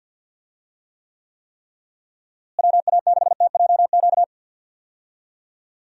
Morse code, heard as a keyed tone, about 46 wpm, 700 Hz. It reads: WA6TJQ
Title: USA Call Signs - 6 Characters - Repeated 3x - 50wpm